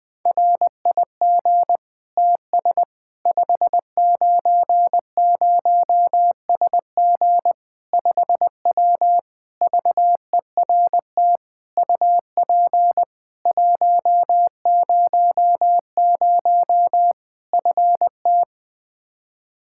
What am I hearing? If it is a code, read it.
Morse code, heard as a keyed tone, about 20 wpm, 700 Hz. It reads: RIG TS 590SG 5W VERT UP 100 FT
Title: Practice QSO - Exchange 2 - Info - 20wpm